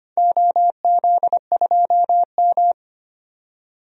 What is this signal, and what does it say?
Morse code, heard as a keyed tone, about 25 wpm, 700 Hz. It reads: OZ2M
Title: DXCC-100 Call Signs - 4 Characters - 25wpm